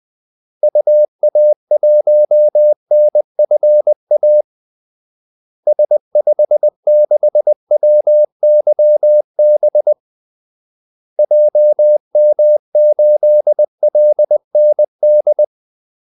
Morse code, heard as a keyed tone, about 20 wpm, 600 Hz. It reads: UA1NFA S56WYB JM8LND
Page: Kaldesignaler 20 wpm